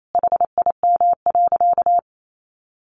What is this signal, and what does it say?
Morse code, heard as a keyed tone, about 28 wpm, 700 Hz. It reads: HIM.
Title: Sentences from Top 1200 Words 28wpm